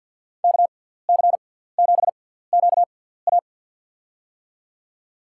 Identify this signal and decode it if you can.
Morse code, heard as a keyed tone, about 50 wpm, spaced as for 17 wpm, 700 Hz. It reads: KC7QA
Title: USA Call Signs - 6x Character Spacing - 50wpm